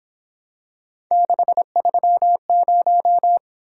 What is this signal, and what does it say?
Morse code, heard as a keyed tone, about 26 wpm, 700 Hz. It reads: 630
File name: Speed-Racing - Three Letter-Number Combinations - 22wpm